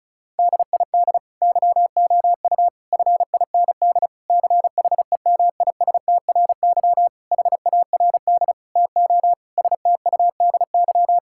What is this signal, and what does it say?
Morse code, heard as a keyed tone, about 34 wpm, 700 Hz. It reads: DID YOU FIND CHEMISTRY HARD TO STUDY
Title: Sentences from Top 1000 Words - Review Part 6 - 35wpm